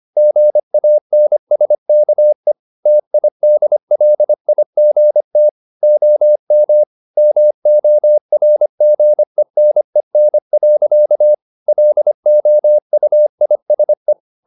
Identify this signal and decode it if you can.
Morse code, heard as a keyed tone, about 25 wpm, 600 Hz. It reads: GANSKE TIDLIGT OM MORGENEN. LOUISE
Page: Ferie 25 wpm